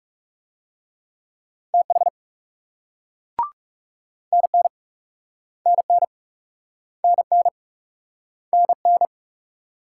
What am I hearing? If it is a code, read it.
Morse code, heard as a keyed tone, about 37 wpm, 700 Hz. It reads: TH NN NN NN NN
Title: Speed-Racing Multi-letter Phonemes - 30wpm